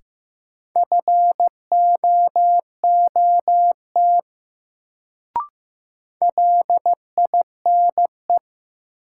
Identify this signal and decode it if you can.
Morse code, heard as a keyed tone, about 15 wpm, 700 Hz. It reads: FOOT LINE
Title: Sets of 5 Words - New Word Drill - 15wpm